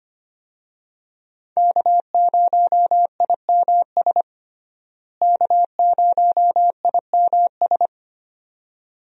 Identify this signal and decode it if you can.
Morse code, heard as a keyed tone, about 25 wpm, 700 Hz. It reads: K0IMS K0IMS
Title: USA Call Signs - 5 Characters - Repeated 2x - 25wpm